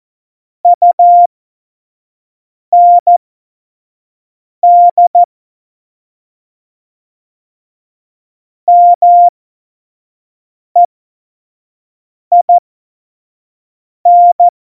Morse code, heard as a keyed tone, about 14 wpm, spaced as for 5 wpm, 700 Hz. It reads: UND MEIN
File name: Morsekurs / Witze / Witz 04 August 2020, 25 F70